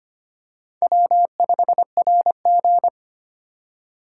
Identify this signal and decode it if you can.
Morse code, heard as a keyed tone, about 25 wpm, 700 Hz. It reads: W5RG
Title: USA Call Signs - 4 Characters - 8x Character Spacing - 25wpm